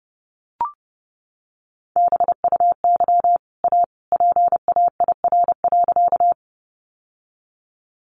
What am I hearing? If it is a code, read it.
Morse code, heard as a keyed tone, about 30 wpm, 700 Hz. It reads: BUY A PAIR.
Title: Sentences from Top 1200 Words 30wpm